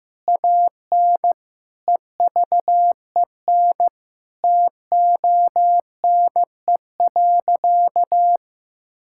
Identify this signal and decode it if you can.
Morse code, heard as a keyed tone, about 15 wpm, 700 Hz. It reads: AN EVEN TONE.
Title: Sentences from Top 1600 Words 15wpm